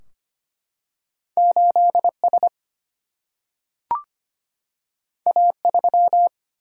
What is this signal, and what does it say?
Morse code, heard as a keyed tone, about 25 wpm, 700 Hz. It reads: 8S A3
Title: Single-Letter Single-Number 25wpm